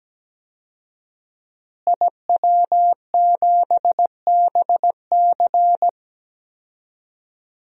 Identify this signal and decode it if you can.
Morse code, heard as a keyed tone, about 17 wpm, 700 Hz. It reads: IW7BC
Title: DXCC-100 Call Signs - 5 Characters - Repeated 3x - 17wpm